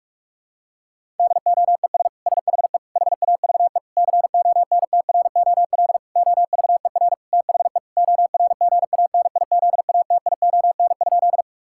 Morse code, heard as a keyed tone, about 41 wpm, 700 Hz. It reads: DOES SHE HAVE CONTROL OVER THE ORGANIZATION?
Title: Sentences from Top 1000 Words - Review Part 5 - 45wpm